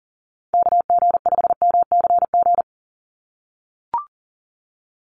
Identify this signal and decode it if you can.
Morse code, heard as a keyed tone, about 39 wpm, 700 Hz. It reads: KG5MCG
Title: USA Call Signs 40wpm